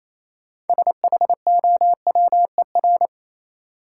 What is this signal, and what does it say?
Morse code, heard as a keyed tone, about 28 wpm, 700 Hz. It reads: SHOWER
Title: Sentences from Top 1600 Words - New Word Drill - 28wpm